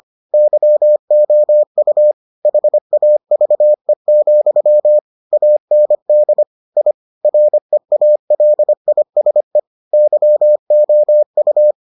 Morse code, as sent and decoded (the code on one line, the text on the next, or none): -.-- --- ..- / .... .- ...- . --..-- / .- -. -.. / .. / .-. .- .-.. .. ... / -.-- --- ..-
YOU HAVE, AND I REALISE YOU